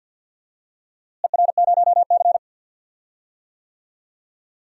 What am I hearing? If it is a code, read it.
Morse code, heard as a keyed tone, about 52 wpm, 700 Hz. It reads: ER0C
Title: DXCC-100 Call Signs - 4 Characters - Repeated 3x - 50wpm